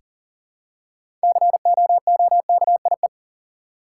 Morse code, heard as a keyed tone, about 40 wpm, 700 Hz. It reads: COOKIE
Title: Six Letter Words - 40wpm